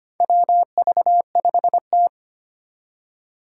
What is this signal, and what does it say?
Morse code, heard as a keyed tone, about 25 wpm, 700 Hz. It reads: WV5T
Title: Four Challenging Letter-Number Combinations 25wpm